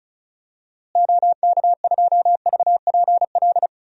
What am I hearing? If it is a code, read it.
Morse code, heard as a keyed tone, about 34 wpm, 700 Hz. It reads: OK2VPL
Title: DXCC-100 Call Signs - 6 Characters - Repeated 2x - 35wpm